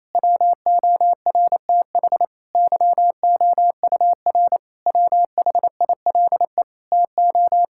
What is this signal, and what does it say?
Morse code, heard as a keyed tone, about 28 wpm, 700 Hz. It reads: WORTH YOUR WHILE TO
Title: Sentences from Top 1600 Words 28wpm